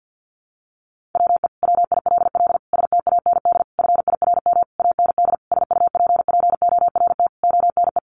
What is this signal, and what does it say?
Morse code, heard as a keyed tone, about 50 wpm, 700 Hz. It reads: WE WILL STAND FIRM AND SUPPORT ONE
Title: Sentences from Top 1400 Words 50wpm